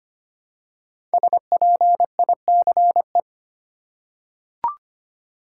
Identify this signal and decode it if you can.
Morse code, heard as a keyed tone, about 25 wpm, 700 Hz. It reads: SPICE
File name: 5 Letter Words - First 3 Letters Given - 25wpm